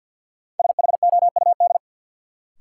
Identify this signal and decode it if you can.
Morse code, heard as a keyed tone, about 51 wpm, 700 Hz. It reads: SHOUD
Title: Sentences from Top 1000 Words - Review Part 4 - New Word Drill - 50wpm